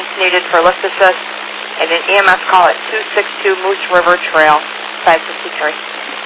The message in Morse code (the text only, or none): none